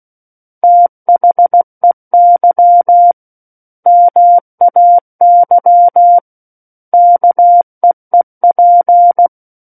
Morse code, heard as a keyed tone, about 16 wpm, 700 Hz. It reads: THEY MAY KEEP
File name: Book One / 07-Chapter 7 - 16 WPM